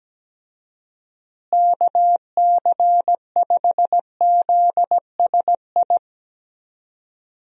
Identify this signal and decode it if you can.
Morse code, heard as a keyed tone, about 17 wpm, 700 Hz. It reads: KC5ZSI
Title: USA Call Signs - 6 Characters - Repeated 2x - 17wpm